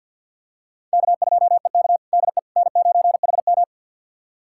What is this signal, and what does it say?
Morse code, heard as a keyed tone, about 52 wpm, 700 Hz. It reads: K2EK DE N9HM
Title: USA Callsign DE Callsign - 4 Characters - 50wpm